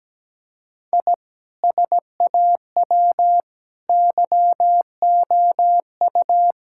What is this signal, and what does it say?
Morse code, heard as a keyed tone, about 17 wpm, 700 Hz. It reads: I SAW YOU
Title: Sentences from Top 1000 Words - Review Part 5 - 17wpm